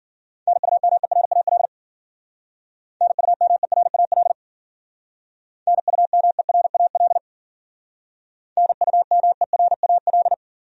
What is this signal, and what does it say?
Morse code, heard as a keyed tone, about 52 wpm, 700 Hz. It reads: NUMERAL NUMERAL NUMERAL NUMERAL